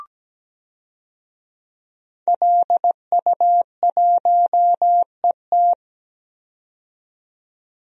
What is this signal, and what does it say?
Morse code, heard as a keyed tone, about 17 wpm, 700 Hz. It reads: LU1ET
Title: DXCC-100 Call Signs - 5 Characters - 17wpm